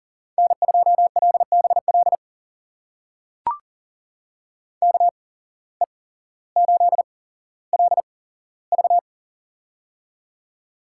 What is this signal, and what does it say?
Morse code, heard as a keyed tone, about 40 wpm, spaced as for 17 wpm, 700 Hz. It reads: N2LBL KE8LV
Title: USA Call Signs - 5 Characters - 8x Character Spacing - 40wpm